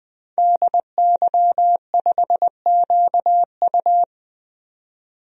Morse code, heard as a keyed tone, about 20 wpm, 700 Hz. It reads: DY5QU
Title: DXCC-100 Call Signs - 5 Characters - 20wpm